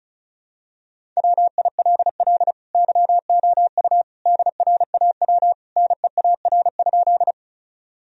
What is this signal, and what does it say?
Morse code, heard as a keyed tone, about 34 wpm, 700 Hz. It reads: WILL YOU DRAW NEAR?
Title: Sentences from Top 1000 Words - Review Part 1 - 35wpm